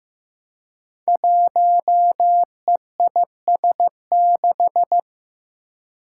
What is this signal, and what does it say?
Morse code, heard as a keyed tone, about 15 wpm, 700 Hz. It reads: 1EIS6
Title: Five Challenging Letter-Number Combinations 15wpm